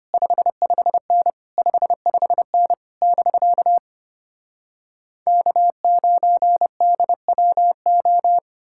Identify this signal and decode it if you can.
Morse code, heard as a keyed tone, about 33 wpm, 700 Hz. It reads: 55N 55N <BK> K9DWO